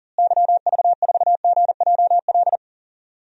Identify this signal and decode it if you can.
Morse code, heard as a keyed tone, about 40 wpm, 700 Hz. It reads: YV4GJL
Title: DXCC-100 Call Signs - 6 Characters - Repeated 2x - 40wpm